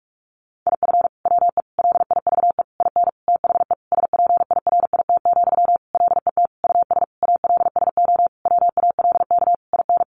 Morse code, heard as a keyed tone, about 44 wpm, 700 Hz. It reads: IF WE LIVE IN THE SPIRIT, LET US ALSO WALK IN
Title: Sentences from Top 1400 Words 45wpm